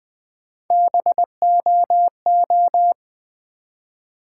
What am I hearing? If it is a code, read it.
Morse code, heard as a keyed tone, about 20 wpm, 700 Hz. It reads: BOO
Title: Build A Word - 1 Letter At A Time - 4-Letter Words - 20wpm